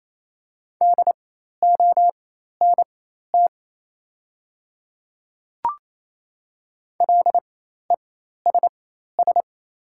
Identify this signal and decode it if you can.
Morse code, heard as a keyed tone, about 28 wpm, spaced as for 13 wpm, 700 Hz. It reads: DONT LESS
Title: 4x Character Spacing - Top 500 Words - 28wpm